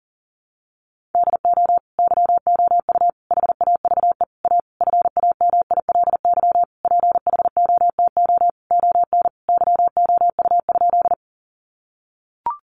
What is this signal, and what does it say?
Morse code, heard as a keyed tone, about 40 wpm, 700 Hz. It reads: DO YOU HAVE A FAMILY PHOTO ON YOU?
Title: Sentences from Top 1200 Words 40wpm